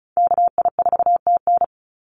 Morse code, heard as a keyed tone, about 34 wpm, 700 Hz. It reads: KI4TN
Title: NA SOTA Activator Callsigns 35wpm